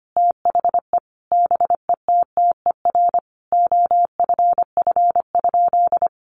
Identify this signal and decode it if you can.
Morse code, heard as a keyed tone, about 25 wpm, 700 Hz. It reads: THE BETTER OFF?
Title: Sentences from Top 500 Words 25wpm